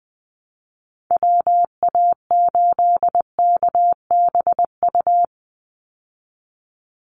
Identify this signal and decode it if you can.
Morse code, heard as a keyed tone, about 20 wpm, 700 Hz. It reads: WA8KBU
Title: Call Signs Repeated 3x - 20wpm